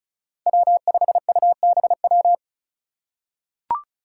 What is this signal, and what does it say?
Morse code, heard as a keyed tone, about 36 wpm, 700 Hz. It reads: W5UBW
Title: Five Challenging Letter-Number Combinations 35wpm